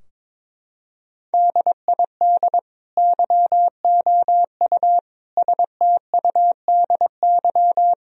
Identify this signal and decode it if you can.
Morse code, heard as a keyed tone, about 21 wpm, 700 Hz. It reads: DID YOU STUDY
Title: Sentences from Top 1000 Words - Review Part 3 - 22wpm